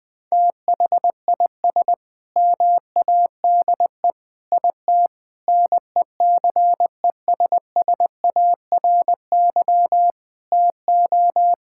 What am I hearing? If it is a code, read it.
Morse code, heard as a keyed tone, about 21 wpm, 700 Hz. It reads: THIS MADE IT NECESSARY TO
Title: Sentences from Top 1600 Words 20wpm